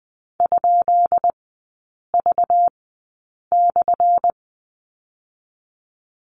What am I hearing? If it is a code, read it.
Morse code, heard as a keyed tone, about 20 wpm, spaced as for 8 wpm, 690 Hz. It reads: ?V/
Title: 198 Lesson 32e - Letter k - Three Characters 20wpm